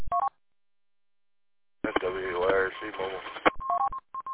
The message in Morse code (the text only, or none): none